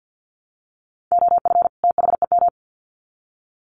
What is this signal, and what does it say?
Morse code, heard as a keyed tone, about 50 wpm, 700 Hz. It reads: OF THEM
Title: Sets of 2 Words 50wpm